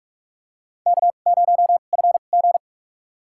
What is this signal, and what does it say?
Morse code, heard as a keyed tone, about 44 wpm, spaced as for 32 wpm, 700 Hz. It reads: K0FG
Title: USA Call Signs - 4 Characters - 2x Character Spacing - 45wpm